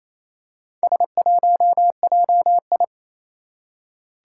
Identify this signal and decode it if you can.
Morse code, heard as a keyed tone, about 28 wpm, 700 Hz. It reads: S1JI
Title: Four Challenging Letter-Number Combinations 28wpm